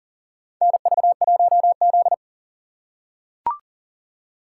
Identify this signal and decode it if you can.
Morse code, heard as a keyed tone, about 40 wpm, 700 Hz. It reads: NV1Z
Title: USA Call Signs - 4 Characters - 4x Character Spacing - 40wpm